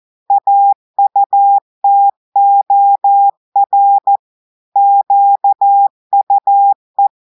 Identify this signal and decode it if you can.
Morse code, heard as a keyed tone, about 14 wpm, 800 Hz. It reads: AUTOR QUE
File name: audios / portugais / portugais texte en clair 1 vitesse 14 mots minutes